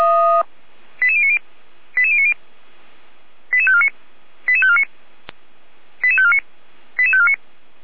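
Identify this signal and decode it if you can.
Five-tone selective call (ZVEI), heard as quick runs of five tones, tones of about 70 ms. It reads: 88090 88090 88549 88549 88549 88549
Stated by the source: ZVEI five-tone selective call received by SDR at fire brigade Orlamünde (feuerwehr-orlamuende).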